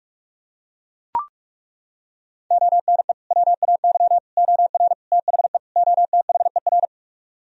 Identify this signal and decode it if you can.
Morse code, heard as a keyed tone, about 41 wpm, 700 Hz. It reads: ONE WAY OR THE OTHER